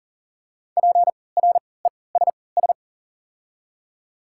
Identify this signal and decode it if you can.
Morse code, heard as a keyed tone, about 40 wpm, spaced as for 23 wpm, 690 Hz. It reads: PRESS
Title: 3x Character Spacing - Top 500 Words - 40wpm